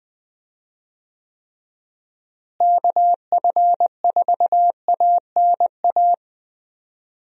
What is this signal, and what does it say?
Morse code, heard as a keyed tone, about 20 wpm, 700 Hz. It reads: KF4ANA
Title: USA Call Signs - 6 Characters - Repeated 3x - 20wpm